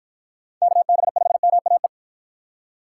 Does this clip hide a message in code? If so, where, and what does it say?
Yes: Morse code from 0.6 to 1.9 s, heard as a keyed tone, about 51 wpm, 700 Hz. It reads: KB5MAE